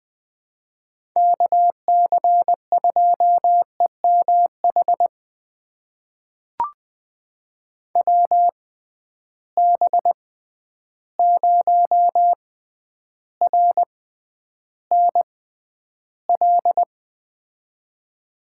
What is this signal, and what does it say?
Morse code, heard as a keyed tone, about 20 wpm, spaced as for 10 wpm, 700 Hz. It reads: KC2EMH WB0RNL
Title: USA Call Signs - 6 Characters - 6x Character Spacing - 20wpm